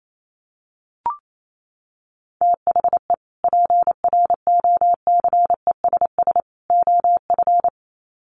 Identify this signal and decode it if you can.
Morse code, heard as a keyed tone, about 28 wpm, 700 Hz. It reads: THE PROCESS OF
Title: Sets of 3 Words 28wpm